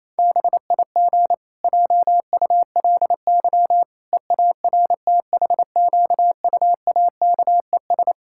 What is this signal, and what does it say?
Morse code, heard as a keyed tone, about 28 wpm, 700 Hz. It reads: BIG JULY EARTHQUAKES
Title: Pangram Challenge v2 - 28wpm